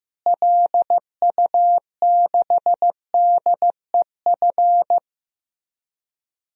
Morse code, heard as a keyed tone, about 15 wpm, 700 Hz. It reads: LU6DEF
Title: DXCC-100 Call Signs - 6 Characters - Repeated 2x - 15wpm